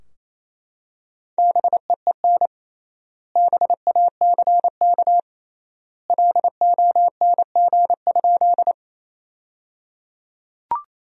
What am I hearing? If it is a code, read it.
Morse code, heard as a keyed tone, about 28 wpm, 700 Hz. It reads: BEEN BACK LONG?
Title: 3x Word-Spacing - Sentences from Top 200 Words - 28wpm